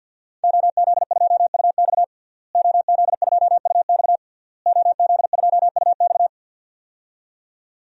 Morse code, heard as a keyed tone, about 52 wpm, 700 Hz. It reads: OZ2UX OZ2UX OZ2UX
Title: DXCC-100 Call Signs - 5 Characters - Repeated 3x - 50wpm